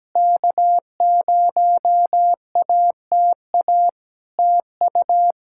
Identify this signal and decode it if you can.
Morse code, heard as a keyed tone, about 17 wpm, 700 Hz. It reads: K0ATA TU
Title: Practice QSO - Exchange 4 - Ending - 17wpm